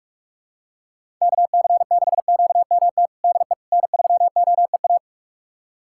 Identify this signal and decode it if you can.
Morse code, heard as a keyed tone, about 46 wpm, 700 Hz. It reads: KC6QMT DE N3OEA
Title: USA Callsign DE Callsign 45wpm